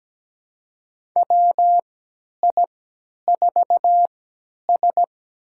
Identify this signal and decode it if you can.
Morse code, heard as a keyed tone, about 17 wpm, spaced as for 10 wpm, 700 Hz. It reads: WI4S
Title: USA Call Signs - 4 Characters - 3x Character Spacing - 17wpm